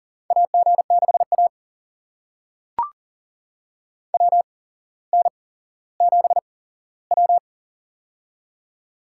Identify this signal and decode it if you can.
Morse code, heard as a keyed tone, about 39 wpm, spaced as for 17 wpm, 690 Hz. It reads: AG6A WN7W
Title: USA Call Signs - 4 Characters - 8x Character Spacing - 40wpm